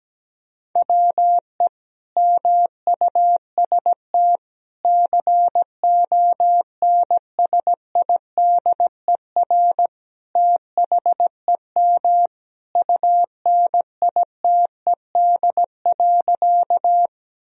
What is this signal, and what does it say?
Morse code, heard as a keyed tone, about 17 wpm, 700 Hz. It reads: WE MUST CONSIDER THEM UNITED.
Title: Sentences from Top 1000 Words - Review Part 4 - 17wpm